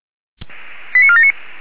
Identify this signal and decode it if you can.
Five-tone selective call (ZVEI), heard as quick runs of five tones, tones of about 70 ms. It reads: 89479